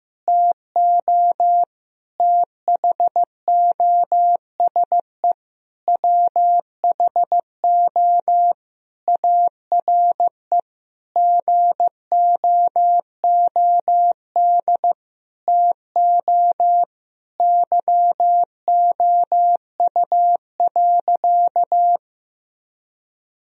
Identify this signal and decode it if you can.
Morse code, heard as a keyed tone, about 15 wpm, 700 Hz. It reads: TO THOSE WHO ARE GOOD TO YOU.
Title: Sentences from Top 1000 Words - Review Part 1 - 15wpm